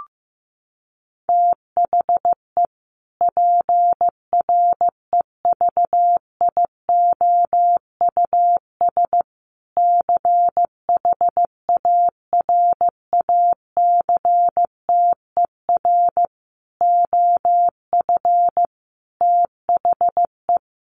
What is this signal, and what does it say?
Morse code, heard as a keyed tone, about 15 wpm, 700 Hz. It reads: THE PREVIOUS CHARACTER OF THE
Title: Sentences from Top 1500 Words 15wpm